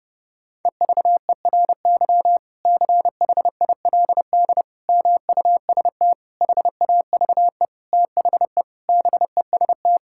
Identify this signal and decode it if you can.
Morse code, heard as a keyed tone, about 32 wpm, 700 Hz. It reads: EVERY CHILD MUST HAVE THE BEST
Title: Sentences from Top 1000 Words - Review Part 6 - 30wpm